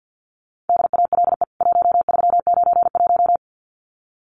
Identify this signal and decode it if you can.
Morse code, heard as a keyed tone, about 50 wpm, 700 Hz. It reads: DALE 1391
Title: CWT Exchange 50wpm